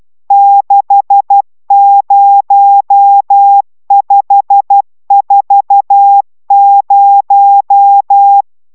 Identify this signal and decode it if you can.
Morse code, heard as a keyed tone, about 12 wpm, 800 Hz. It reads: 60540
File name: taviro tanulas / cw 69 20 10